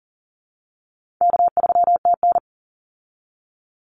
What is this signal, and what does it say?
Morse code, heard as a keyed tone, about 40 wpm, 700 Hz. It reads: K3TN
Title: Easy Call Signs Repeated 2x - 40wpm